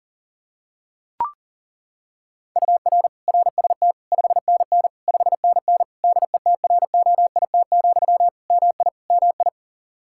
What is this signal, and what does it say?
Morse code, heard as a keyed tone, about 40 wpm, 700 Hz. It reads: UR RST 5NN 5NN DETROIT, MI MI
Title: Practice QSO - Exchange 1 - Beginning - 40wpm